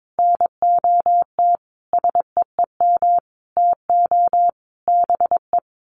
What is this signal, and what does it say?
Morse code, heard as a keyed tone, about 22 wpm, 700 Hz. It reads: NOT SEEM TO BE
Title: Sets of 5 Words Encore2 22wpm